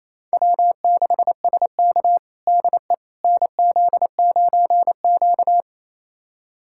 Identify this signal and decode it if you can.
Morse code, heard as a keyed tone, about 27 wpm, 700 Hz. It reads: W6SK DE NZ9Q
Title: USA Callsign DE Callsign - 4 Characters - 28wpm